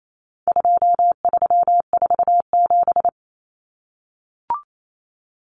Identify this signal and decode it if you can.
Morse code, heard as a keyed tone, about 28 wpm, 700 Hz. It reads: 2347
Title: Numbers - 4-Digit 28wpm